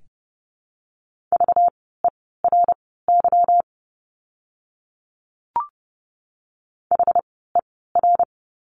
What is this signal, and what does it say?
Morse code, heard as a keyed tone, about 30 wpm, spaced as for 17 wpm, 700 Hz. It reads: VERY HER
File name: course / 30:10wpm / 230 Lesson 37f - Letter x - Words 30wpm